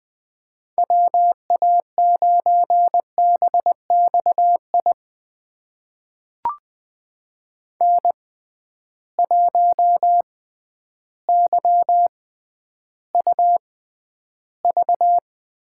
Morse code, heard as a keyed tone, about 20 wpm, spaced as for 11 wpm, 700 Hz. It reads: WA9BXI N1YUV
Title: USA Call Signs - 6x Character Spacing - 20wpm